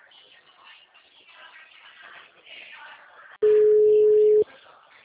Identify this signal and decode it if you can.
Ringback tone (Europe, 425 Hz), with speech.